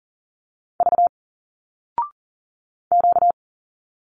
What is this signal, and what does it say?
Morse code, heard as a keyed tone, about 40 wpm, 700 Hz. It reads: V Q
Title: Single Letters 40wpm